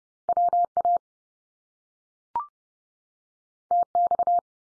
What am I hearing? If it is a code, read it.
Morse code, heard as a keyed tone, about 30 wpm, 700 Hz. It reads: WA TX
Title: US States 30WPM